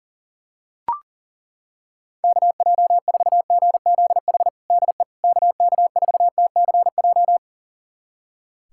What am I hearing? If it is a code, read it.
Morse code, heard as a keyed tone, about 40 wpm, 700 Hz. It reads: KJ4GZH DE KK4TCJ
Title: USA Callsign DE Callsign - 6 Characters - 40wpm